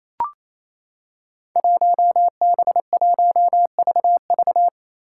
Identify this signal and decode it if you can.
Morse code, heard as a keyed tone, about 28 wpm, 700 Hz. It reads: 1B1VV
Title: Five Challenging Letter-Number Combinations 28wpm